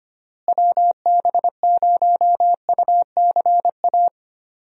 Morse code, heard as a keyed tone, about 25 wpm, 700 Hz. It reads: WB0UCA
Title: USA Call Signs - 6 Characters - Repeated 2x - 25wpm